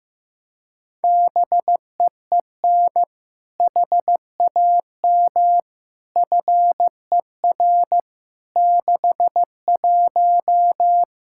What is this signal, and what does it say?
Morse code, heard as a keyed tone, about 15 wpm, 700 Hz. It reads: BEEN HAM FER 61
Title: K1USN SST QSO Practice 15wpm